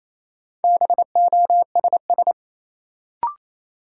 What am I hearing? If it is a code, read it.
Morse code, heard as a keyed tone, about 28 wpm, 700 Hz. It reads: BOSS
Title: Sentences from Top 700 Words - New Word Drill - 28wpm